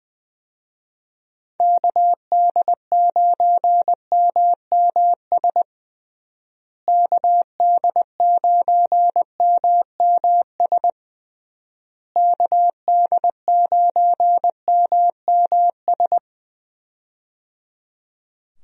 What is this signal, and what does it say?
Morse code, heard as a keyed tone, about 20 wpm, 700 Hz. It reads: KD9MMS KD9MMS KD9MMS
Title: USA Call Signs - 6 Characters - Repeated 3x - 20wpm